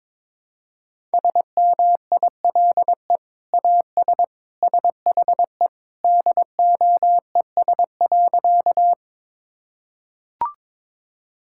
Morse code, heard as a keyed tone, about 22 wpm, 700 Hz. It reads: SMILE AS SHE DOES.
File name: practice / 22wpm / Sentences with 2-Word Phrases - 22wpm